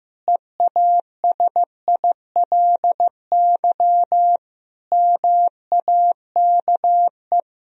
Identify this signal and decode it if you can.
Morse code, heard as a keyed tone, about 15 wpm, 700 Hz. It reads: EASILY MAKE
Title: Sentences from Top 1000 Words - Review Part 1 - 15wpm